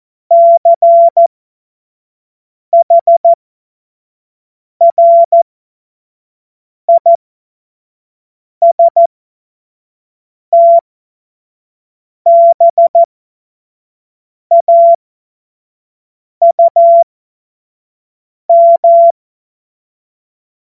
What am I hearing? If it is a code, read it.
Morse code, heard as a keyed tone, about 14 wpm, spaced as for 5 wpm, 680 Hz. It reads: CHRISTBAUM
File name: Morsekurs / Witze / Witz 14. September 2021, 25 F70